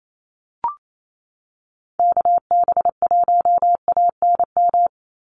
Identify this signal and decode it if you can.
Morse code, heard as a keyed tone, about 28 wpm, 700 Hz. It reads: KB1ANM
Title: USA Call Signs 28wpm